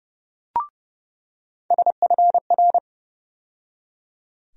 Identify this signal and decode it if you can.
Morse code, heard as a keyed tone, about 30 wpm, 700 Hz. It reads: SFR
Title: USA Call Sign Postfixes - 3 Letters 30wpm